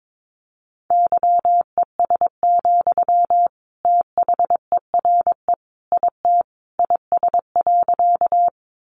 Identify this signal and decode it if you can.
Morse code, heard as a keyed tone, about 22 wpm, 700 Hz. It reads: YES, THERE IT IS.
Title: Sentences from Top 500 Words 22wpm